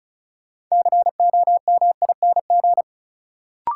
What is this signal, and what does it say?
Morse code, heard as a keyed tone, about 36 wpm, 700 Hz. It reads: COMING
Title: Sentences from Top 1300 Words - New Word Drill - 35wpm